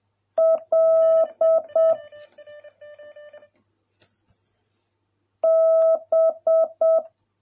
Morse code, heard as a keyed tone, about 7 wpm, 650 Hz. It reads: L B